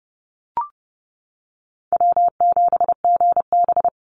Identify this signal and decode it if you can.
Morse code, heard as a keyed tone, about 30 wpm, 700 Hz. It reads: W7GB
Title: Easy Call Signs - 30wpm